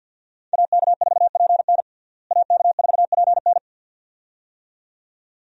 Morse code, heard as a keyed tone, about 52 wpm, 700 Hz. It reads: AK4PN AK4PN
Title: USA Call Signs - 5 Characters - Repeated 2x - 50wpm